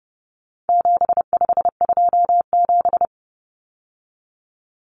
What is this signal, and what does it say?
Morse code, heard as a keyed tone, about 30 wpm, 700 Hz. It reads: 7527
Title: Numbers - 4-Digit 30wpm